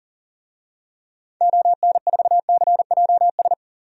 Morse code, heard as a keyed tone, about 37 wpm, 700 Hz. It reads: ON4CJS
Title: DXCC-100 Call Signs - 6 Characters - 40wpm